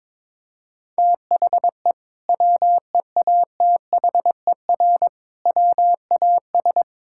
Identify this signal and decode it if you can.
Morse code, heard as a keyed tone, about 22 wpm, 700 Hz. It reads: THE WEATHER WAS
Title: Sentences from Top 1000 Words - Review Part 5 - 22wpm